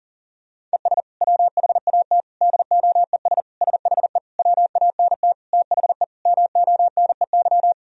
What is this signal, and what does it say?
Morse code, heard as a keyed tone, about 40 wpm, 700 Hz. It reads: ES WHAT DOES SHE WANT THE MONEY
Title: Sentences from Top 400 Words and Abbreviations 40wpm